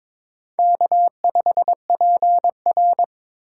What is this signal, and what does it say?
Morse code, heard as a keyed tone, about 22 wpm, 700 Hz. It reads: K5PR
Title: USA Call Signs - 4x Character Spacing - 22wpm